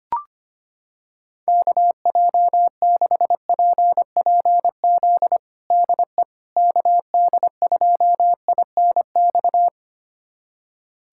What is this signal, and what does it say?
Morse code, heard as a keyed tone, about 25 wpm, 700 Hz. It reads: KJ6PPZ DE KD2INX
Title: USA Callsign DE Callsign 25wpm